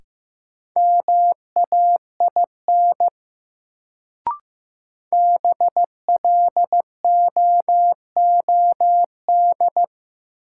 Morse code, heard as a keyed tone, about 16 wpm, 700 Hz. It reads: MAIN BLOOD
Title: Sentences from Top 1000 Words - Review Part 2 - New Word Drill - 15wpm